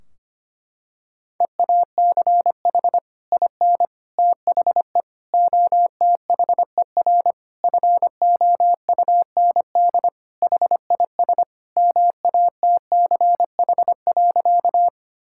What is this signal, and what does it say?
Morse code, heard as a keyed tone, about 25 wpm, 700 Hz. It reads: EACH IN THE OTHER FOUND HIS MATCH.
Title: Sentences from Top 1600 Words 25wpm